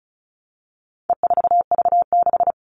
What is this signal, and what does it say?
Morse code, heard as a keyed tone, about 34 wpm, 700 Hz. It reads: E4V6
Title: All Cut Numbers - 4-Digit 35wpm